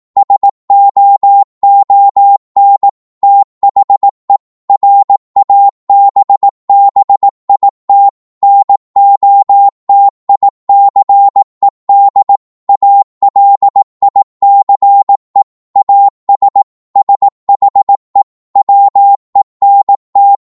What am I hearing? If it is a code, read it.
Morse code, heard as a keyed tone, about 18 wpm, 800 Hz. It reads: SOON THE RABBIT NOTICED ALICE AS SHE WENT